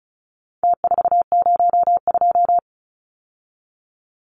Morse code, heard as a keyed tone, about 34 wpm, 700 Hz. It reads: T402
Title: Numbers - Contest Style - 4-Digit 35wpm